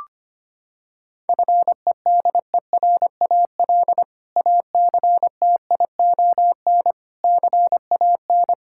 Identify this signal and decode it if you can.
Morse code, heard as a keyed tone, about 25 wpm, 700 Hz. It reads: FEDERAL ACTION CAN
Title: Sentences from Top 1000 Words - Review Part 5 - 25wpm